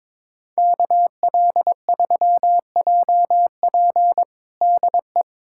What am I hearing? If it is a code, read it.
Morse code, heard as a keyed tone, about 22 wpm, 700 Hz. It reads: KL3JP DE
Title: USA Callsign DE Callsign - 5 Characters - 22wpm